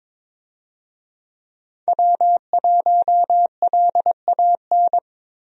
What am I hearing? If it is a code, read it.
Morse code, heard as a keyed tone, about 21 wpm, 700 Hz. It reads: W1LAN
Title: USA Call Signs - 5 Characters - 22wpm